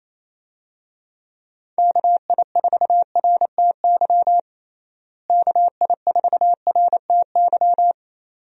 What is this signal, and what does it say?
Morse code, heard as a keyed tone, about 28 wpm, 700 Hz. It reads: KI4RTY KI4RTY
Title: USA Call Signs - 6 Characters - Repeated 2x - 28wpm